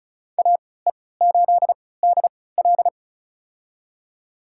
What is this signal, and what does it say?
Morse code, heard as a keyed tone, about 36 wpm, spaced as for 20 wpm, 700 Hz. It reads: AE8DL
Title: USA Call Signs - 5 Characters - 3x Character Spacing - 35wpm